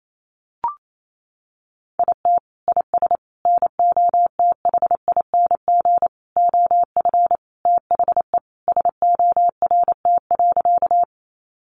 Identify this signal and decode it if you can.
Morse code, heard as a keyed tone, about 28 wpm, 700 Hz. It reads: IT IS NOTHING OF THE SORT.
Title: Sentences from Top 1300 Words 28wpm